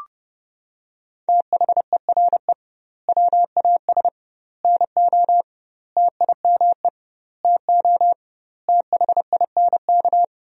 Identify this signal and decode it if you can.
Morse code, heard as a keyed tone, about 30 wpm, 700 Hz. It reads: THERE WAS NO TIME TO THINK
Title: 2x Word-Spacing - Sentences from Top 400 Words - 30wpm